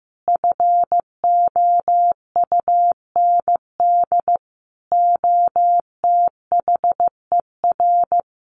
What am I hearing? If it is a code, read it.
Morse code, heard as a keyed tone, about 15 wpm, 700 Hz. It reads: FOUND OTHER
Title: Sentences from Top 1500 Words 15wpm